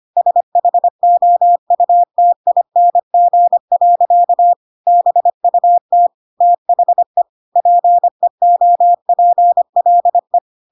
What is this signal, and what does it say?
Morse code, heard as a keyed tone, about 25 wpm, 700 Hz. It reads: SHOUTING. BUT THE PEOPLE
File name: Pinocchio / Pinocchio ch3 25 25 ik1hge